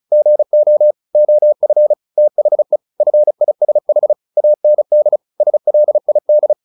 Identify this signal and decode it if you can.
Morse code, heard as a keyed tone, about 34 wpm, 600 Hz. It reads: GO OF THE FISH AND SLID
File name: the old man and the sea 2 35wpm